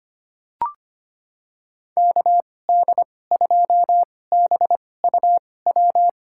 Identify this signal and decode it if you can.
Morse code, heard as a keyed tone, about 25 wpm, spaced as for 18 wpm, 700 Hz. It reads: KD2BUW
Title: USA Call Signs - 6 Characters - 2x Character Spacing - 25wpm